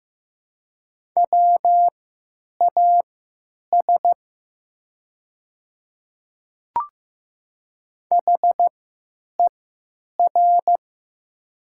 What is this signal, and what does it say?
Morse code, heard as a keyed tone, about 15 wpm, spaced as for 9 wpm, 700 Hz. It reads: WAS HER
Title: 3x Character Spacing - Top 100 Words - 15wpm